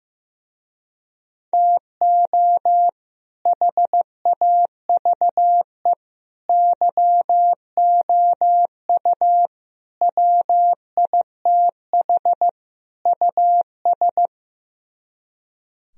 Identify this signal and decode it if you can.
Morse code, heard as a keyed tone, about 16 wpm, 700 Hz. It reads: TO HAVE YOU WITH US
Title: Speed-Racing - Sets of 5 Words 15wpm